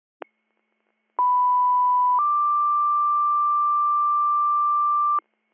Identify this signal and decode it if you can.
Two-tone page, heard as a one-second tone, then a three-second tone, about 990 then 1180 Hz, with noise. It is heard from 1.2 to 5.2 s.